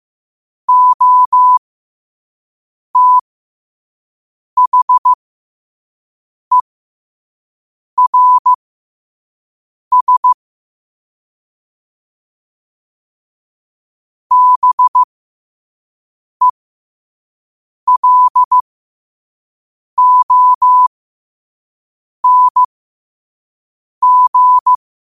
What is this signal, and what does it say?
Morse code, heard as a keyed tone, about 15 wpm, spaced as for 5 wpm, 1000 Hz. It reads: OTHERS BELONG
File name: qotdfiles / 5wpm